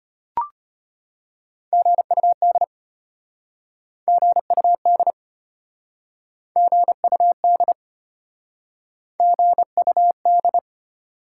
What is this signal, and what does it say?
Morse code, heard as a keyed tone, about 34 wpm, 700 Hz. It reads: GUD GUD GUD GUD